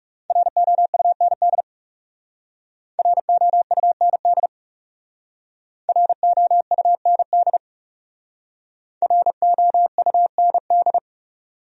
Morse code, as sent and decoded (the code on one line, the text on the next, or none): .-. --- ..- -. -.. / .-. --- ..- -. -.. / .-. --- ..- -. -.. / .-. --- ..- -. -..
ROUND ROUND ROUND ROUND